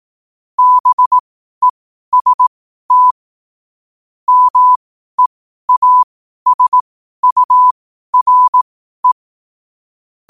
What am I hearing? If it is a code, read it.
Morse code, heard as a keyed tone, about 18 wpm, spaced as for 12 wpm, 1000 Hz. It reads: BEST MEASURE
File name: qotdfiles / 12wpm